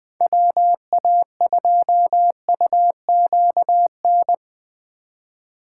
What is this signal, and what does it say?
Morse code, heard as a keyed tone, about 21 wpm, 700 Hz. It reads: WA2UQN
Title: USA Call Signs - 6 Characters - 2x Character Spacing - 20wpm